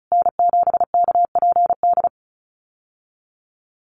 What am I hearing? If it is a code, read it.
Morse code, heard as a keyed tone, about 34 wpm, 700 Hz. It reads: N7KPD
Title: USA Call Signs Repeated 3x 35wpm